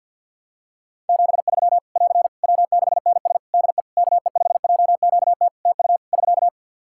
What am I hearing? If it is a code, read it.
Morse code, heard as a keyed tone, about 52 wpm, 700 Hz. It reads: 73 <AR> W6NS DE KE5JQT TU <SK>
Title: Practice QSO - Exchange 4 - Ending - 50wpm